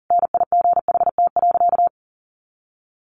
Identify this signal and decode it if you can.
Morse code, heard as a keyed tone, about 40 wpm, 700 Hz. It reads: NIGHT.
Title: Sentences from Top 300 Words 40wpm